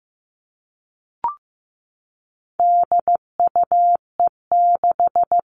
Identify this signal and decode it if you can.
Morse code, heard as a keyed tone, about 15 wpm, 700 Hz. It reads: DUE6
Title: All Cut Numbers - 4-Digit 15wpm